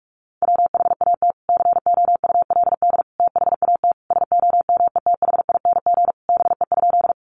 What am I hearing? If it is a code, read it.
Morse code, heard as a keyed tone, about 44 wpm, 700 Hz. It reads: WHAT COULD THAT SOMETHING BE?
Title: Sentences from Top 600 Words 45wpm